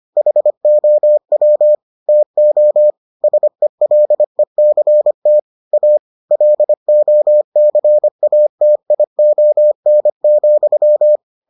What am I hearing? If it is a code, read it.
Morse code, heard as a keyed tone, about 25 wpm, 600 Hz. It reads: HOW TO SELECT A LOCATION,